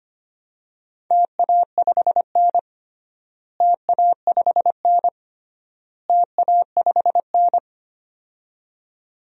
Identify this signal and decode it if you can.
Morse code, heard as a keyed tone, about 25 wpm, 700 Hz. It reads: TA5N TA5N TA5N
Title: DXCC-100 Call Signs - 4 Characters - Repeated 3x - 25wpm